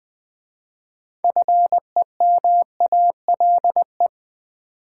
Morse code, heard as a keyed tone, about 20 wpm, 700 Hz. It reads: FEMALE
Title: 6 Letter Words - First 3 Letters Given - 20wpm